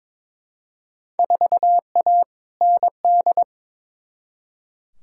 Morse code, heard as a keyed tone, about 22 wpm, 700 Hz. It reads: 4A ND
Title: Field Day Exchanges Encore - 22wpm